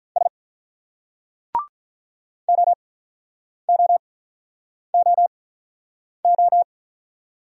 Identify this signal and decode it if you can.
Morse code, heard as a keyed tone, about 48 wpm, spaced as for 9 wpm, 700 Hz. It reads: S OOOO